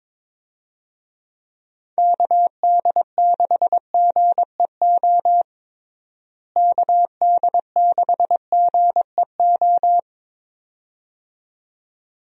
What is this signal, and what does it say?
Morse code, heard as a keyed tone, about 22 wpm, 700 Hz. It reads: KD6GEO KD6GEO
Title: USA Call Signs - 6 Characters - Repeated 2x - 22wpm